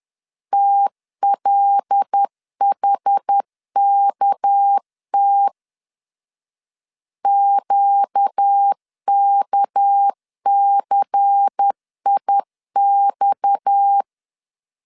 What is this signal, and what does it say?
Morse code, heard as a keyed tone, about 10 wpm, 800 Hz. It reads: TLHKT QKCIX
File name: LEZIONI / 6 PKX0 / 6 PKX 10wpm 2426